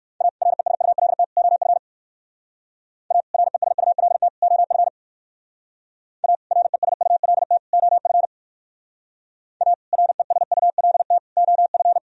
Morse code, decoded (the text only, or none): A RESULT OF A RESULT OF A RESULT OF A RESULT OF